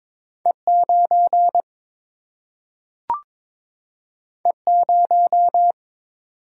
Morse code, heard as a keyed tone, about 22 wpm, 700 Hz. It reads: E 9 E 0
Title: Single-Letter Single-Number 22wpm